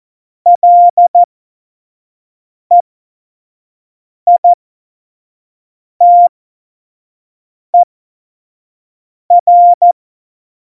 Morse code, heard as a keyed tone, about 14 wpm, spaced as for 5 wpm, 700 Hz. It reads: LEITER
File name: Morsekurs / Witze / Witz 12. Dezember 2017, 25 F70